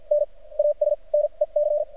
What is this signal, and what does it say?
Morse code, heard as a keyed tone, about 48 wpm, spaced as for 39 wpm, 590 Hz. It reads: N AS NEC
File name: archived / CRSnet